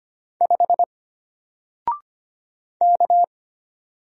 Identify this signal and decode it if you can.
Morse code, heard as a keyed tone, about 25 wpm, 700 Hz. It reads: 5 K
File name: Single-Letter-Number-25wpm